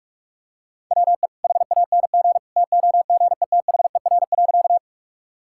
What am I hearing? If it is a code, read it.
Morse code, heard as a keyed tone, about 45 wpm, 700 Hz. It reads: WE HANG TOGETHER.
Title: Sentences from Top 1600 Words 45wpm